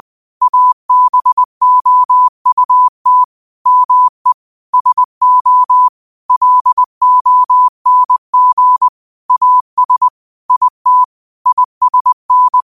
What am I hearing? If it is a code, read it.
Morse code, heard as a keyed tone, about 20 wpm, 1000 Hz. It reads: ABOUT ME SO LONG AS IT ISN
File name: 20wpm